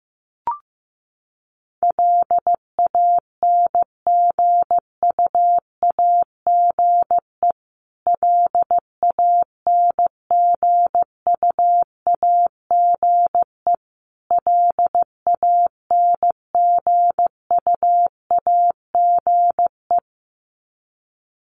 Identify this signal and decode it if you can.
Morse code, heard as a keyed tone, about 15 wpm, 700 Hz. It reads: LANGUAGE LANGUAGE LANGUAGE
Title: Top 500 Words Repeated 15wpm